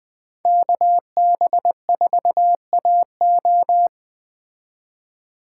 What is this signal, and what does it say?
Morse code, heard as a keyed tone, about 20 wpm, 700 Hz. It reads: KB4AO